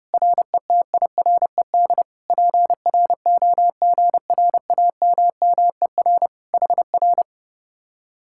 Morse code, heard as a keyed tone, about 28 wpm, 700 Hz. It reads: RETIRED PROGRAMMER HR
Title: Speed-Racing Common QSO Phrases Encore - 30wpm